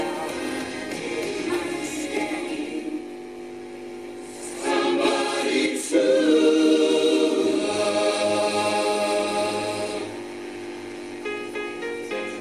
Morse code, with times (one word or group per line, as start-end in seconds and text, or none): none